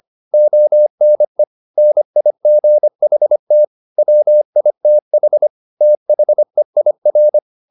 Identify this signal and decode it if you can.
Morse code, heard as a keyed tone, about 25 wpm, 600 Hz. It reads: ONE NIGHT WITH THEIR
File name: the old man and the sea part 1 25wpm